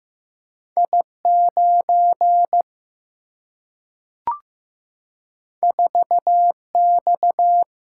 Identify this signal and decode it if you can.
Morse code, heard as a keyed tone, about 15 wpm, 700 Hz. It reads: I9 4X
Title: Single-Letter Single-Number 15wpm